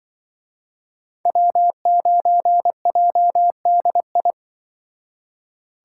Morse code, heard as a keyed tone, about 25 wpm, 700 Hz. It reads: W9JDI